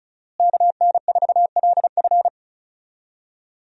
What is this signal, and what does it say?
Morse code, heard as a keyed tone, about 34 wpm, 700 Hz. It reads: KN4LF
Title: USA Call Signs - 5 Characters - Repeated 2x - 35wpm